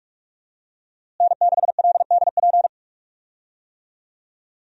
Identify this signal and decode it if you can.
Morse code, heard as a keyed tone, about 46 wpm, 700 Hz. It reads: N6LDP
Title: USA Call Signs - 5 Characters - Repeated 2x - 45wpm